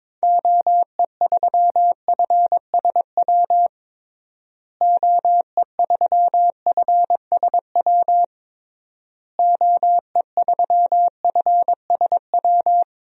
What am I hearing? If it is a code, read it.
Morse code, heard as a keyed tone, about 22 wpm, 700 Hz. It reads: OE3FSW OE3FSW OE3FSW
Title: DXCC-100 Call Signs - 6 Characters - Repeated 3x - 22wpm